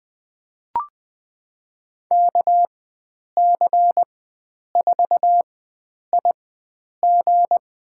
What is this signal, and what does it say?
Morse code, heard as a keyed tone, about 20 wpm, spaced as for 9 wpm, 700 Hz. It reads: KC4IG
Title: USA Call Signs - 4x Character Spacing - 20wpm